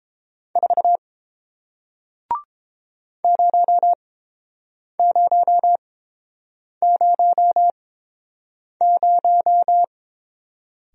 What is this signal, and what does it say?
Morse code, heard as a keyed tone, about 30 wpm, spaced as for 7 wpm, 700 Hz. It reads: 4 0000